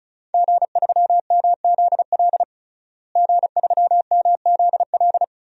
Morse code, heard as a keyed tone, about 36 wpm, 700 Hz. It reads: G3MZL G3MZL
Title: DXCC-100 Call Signs - 5 Characters - Repeated 3x - 35wpm